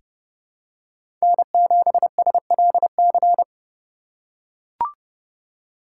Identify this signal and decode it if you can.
Morse code, heard as a keyed tone, about 30 wpm, 700 Hz. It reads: N7SLC
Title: USA Call Signs - 4x Character Spacing - 30wpm